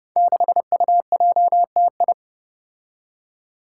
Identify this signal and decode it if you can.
Morse code, heard as a keyed tone, about 30 wpm, 700 Hz. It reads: 6UJTI
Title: Five Challenging Letter-Number Combinations 30wpm